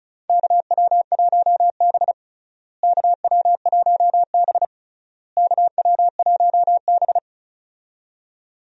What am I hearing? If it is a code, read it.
Morse code, heard as a keyed tone, about 36 wpm, 700 Hz. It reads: KW1B KW1B KW1B
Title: USA Call Signs - 4 Characters - Repeated 3x - 35wpm